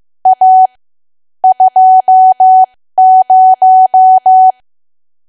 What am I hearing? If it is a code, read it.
Morse code, heard as a keyed tone, about 15 wpm, spaced as for 13 wpm, 750 Hz. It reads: A 20
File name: cwa / W1AW-Level2 / 13WPM